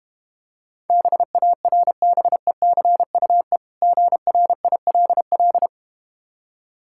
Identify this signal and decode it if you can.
Morse code, heard as a keyed tone, about 31 wpm, 700 Hz. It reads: BARBECUE GRILL
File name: Easy Words - Speed-Racing with Hints - 25wpm